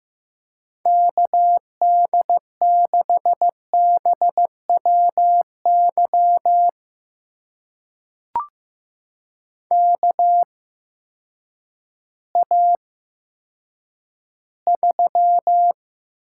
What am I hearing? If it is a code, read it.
Morse code, heard as a keyed tone, about 16 wpm, spaced as for 9 wpm, 700 Hz. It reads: KD6BWY KA3
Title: USA Call Signs - 6 Characters - 8x Character Spacing - 15wpm